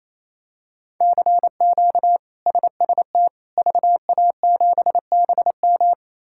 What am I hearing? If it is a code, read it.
Morse code, heard as a keyed tone, about 28 wpm, 700 Hz. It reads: CQ SST VA7BM
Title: K1USN SST QSO Practice - 28wpm